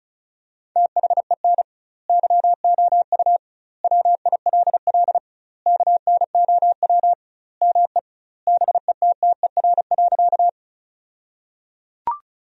Morse code, heard as a keyed tone, about 34 wpm, 700 Hz. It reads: THEN YOU WILL KNOW ME BETTER.
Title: 2x Word-Spacing - Sentences from Top 400 Words - 35wpm